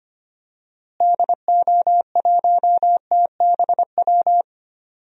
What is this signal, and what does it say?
Morse code, heard as a keyed tone, about 25 wpm, 700 Hz. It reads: DO1TBW
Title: DXCC-100 Call Signs - 6 Characters - Repeated 2x - 25wpm